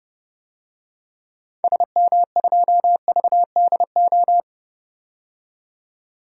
Morse code, heard as a keyed tone, about 32 wpm, 700 Hz. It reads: SM2VDO
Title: DXCC-100 Call Signs - 6 Characters - 30wpm